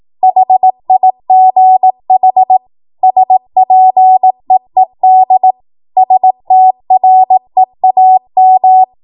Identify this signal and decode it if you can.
Morse code, heard as a keyed tone, about 18 wpm, 750 Hz. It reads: HIGH SPEED STREAM
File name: Morse / 18WPM